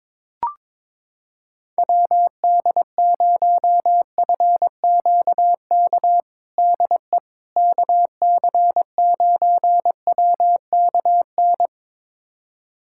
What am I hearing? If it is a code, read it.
Morse code, heard as a keyed tone, about 21 wpm, 700 Hz. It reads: WD0FQK DE KC9WKN
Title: USA Callsign DE Callsign 22wpm